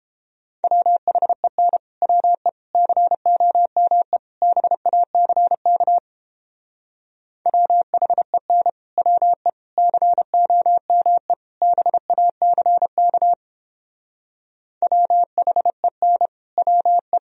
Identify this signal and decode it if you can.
Morse code, heard as a keyed tone, about 29 wpm, 700 Hz. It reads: WHEN WE COME BACK WHEN WE COME BACK WHEN WE